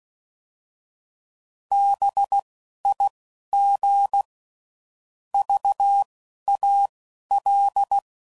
Morse code, heard as a keyed tone, about 16 wpm, spaced as for 11 wpm, 800 Hz. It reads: BIG VAL
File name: CW Code Groups @10WPM / fromSM's / Cities